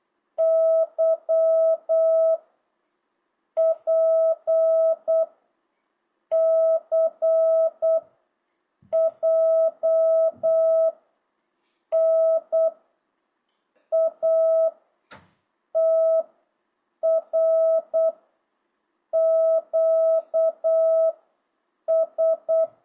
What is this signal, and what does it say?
Morse code, heard as a keyed tone, about 8 wpm, 660 Hz. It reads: Y P C J N A T R Q S